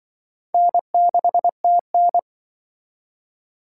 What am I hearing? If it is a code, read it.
Morse code, heard as a keyed tone, about 25 wpm, 700 Hz. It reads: N6TN